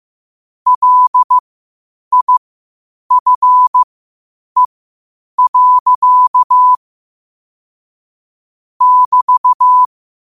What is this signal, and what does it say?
Morse code, heard as a keyed tone, about 15 wpm, spaced as for 8 wpm, 1000 Hz. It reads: LIFE. <BT>
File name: qotdfiles / 8wpm